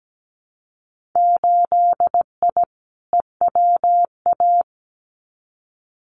Morse code, heard as a keyed tone, about 17 wpm, 700 Hz. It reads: 8I EWA
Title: Winter Field Day Exchanges Encore 17wpm